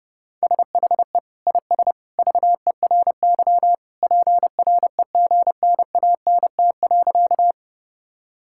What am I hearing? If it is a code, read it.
Morse code, heard as a keyed tone, about 30 wpm, 700 Hz. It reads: SHE IS VERY PREGNANT.
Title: Sentences from Top 1000 Words - Review Part 6 - 30wpm